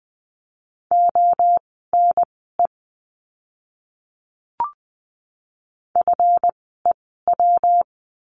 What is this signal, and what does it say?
Morse code, heard as a keyed tone, about 20 wpm, spaced as for 14 wpm, 700 Hz. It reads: ONE FEW
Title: 139 Lesson 23f - Letter f - Words 20wpm